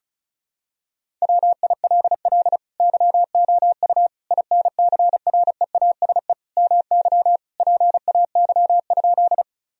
Morse code, heard as a keyed tone, about 34 wpm, 700 Hz. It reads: WILL YOU INCREASE MY PAY?
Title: Sentences from Top 1000 Words - Review Part 4 - 35wpm